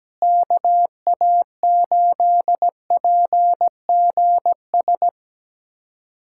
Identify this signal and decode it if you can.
Morse code, heard as a keyed tone, about 17 wpm, 700 Hz. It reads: KA8PGS
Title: USA Callsign DE Callsign - 6 Characters - 17wpm